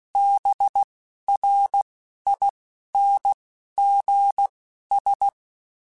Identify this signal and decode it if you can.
Morse code, heard as a keyed tone, about 16 wpm, spaced as for 12 wpm, 800 Hz. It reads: BRINGS
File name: Quotes / Quotes-mp3s / Persperation @10